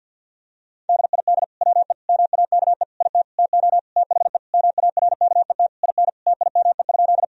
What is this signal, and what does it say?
Morse code, heard as a keyed tone, about 50 wpm, 700 Hz. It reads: DID WE MAKE IT TO THE MARKET IN TIME?
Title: Sentences from Top 1000 Words - Review Part 3 - 50wpm